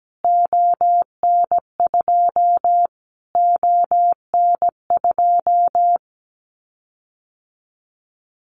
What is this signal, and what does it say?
Morse code, heard as a keyed tone, about 17 wpm, 700 Hz. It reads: ON2 ON2
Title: Callsign Prefixes for Top 25 DXCC - Repeated 3x - 17wpm